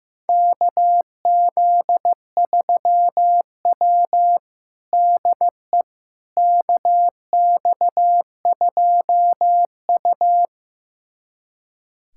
Morse code, heard as a keyed tone, about 15 wpm, 700 Hz. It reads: KZ3W DE KX2U
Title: USA Callsign DE Callsign - 4 Characters - 15wpm